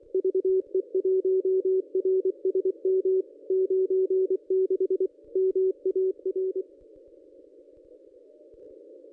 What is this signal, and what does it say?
Morse code, heard as a keyed tone, about 24 wpm, 380 Hz. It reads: VE1RSM 96 MAR
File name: contests / ss-cw-k5tr-op-n5rz -1100